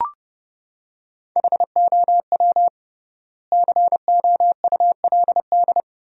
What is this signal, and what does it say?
Morse code, heard as a keyed tone, about 32 wpm, 700 Hz. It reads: HOW COULD
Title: 3x Word-Spacing - Sentences from Top 400 Words - 30wpm